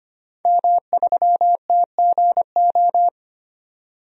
Morse code, heard as a keyed tone, about 25 wpm, 700 Hz. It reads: M3TGO
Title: DXCC-100 Call Signs - 5 Characters - 25wpm